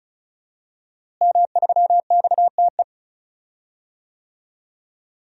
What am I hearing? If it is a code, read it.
Morse code, heard as a keyed tone, about 36 wpm, 700 Hz. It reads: M3XTE
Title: DXCC-100 Call Signs - 5 Characters - Repeated 3x - 35wpm